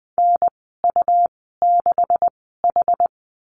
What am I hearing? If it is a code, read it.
Morse code, heard as a keyed tone, about 20 wpm, spaced as for 14 wpm, 700 Hz. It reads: NU6H
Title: 161 Lesson 26g - Letter g - Call Signs 20wpm